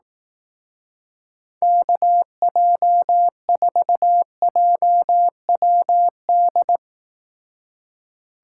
Morse code, heard as a keyed tone, about 19 wpm, 700 Hz. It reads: KJ4JWD
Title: Speed Racing - USA 2x3 Call Signs 15wpm